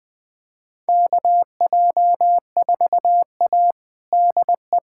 Morse code, heard as a keyed tone, about 21 wpm, 700 Hz. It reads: KJ4A DE
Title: USA Callsign DE Callsign - 4 Characters - 20wpm